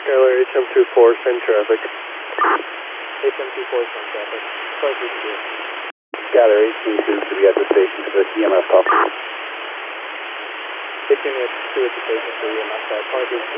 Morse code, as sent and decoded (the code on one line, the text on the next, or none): none